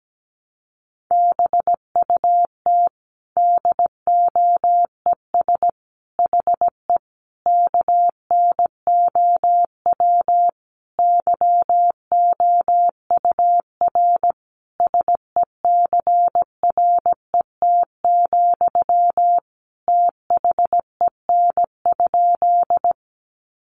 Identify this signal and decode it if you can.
Morse code, heard as a keyed tone, about 17 wpm, 700 Hz. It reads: BUT DOES HE KNOW YOUR SECRET, THEN?
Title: Sentences from Top 1200 Words 17wpm